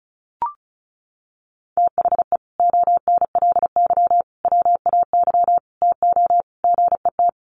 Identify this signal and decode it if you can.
Morse code, heard as a keyed tone, about 34 wpm, 700 Hz. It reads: THE ONLY WAY TO GET
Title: Sets of 5 Words Encore2 35wpm